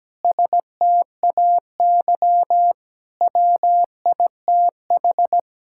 Morse code, heard as a keyed tone, about 17 wpm, 700 Hz. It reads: STAY WITH
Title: Sentences from Top 1600 Words 17wpm